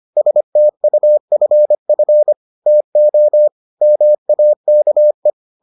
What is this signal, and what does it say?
Morse code, heard as a keyed tone, about 25 wpm, 600 Hz. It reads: STUFF TO MAKE